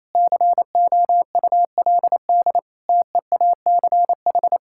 Morse code, heard as a keyed tone, about 27 wpm, 700 Hz. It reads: COULD TEACH
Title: Sentences from Top 1000 Words - Review Part 6 - 28wpm